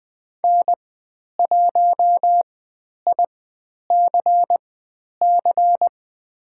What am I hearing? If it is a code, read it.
Morse code, heard as a keyed tone, about 20 wpm, spaced as for 10 wpm, 700 Hz. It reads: N1ICC
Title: 109 Lesson 18g - Letter m - Call Signs 20wpm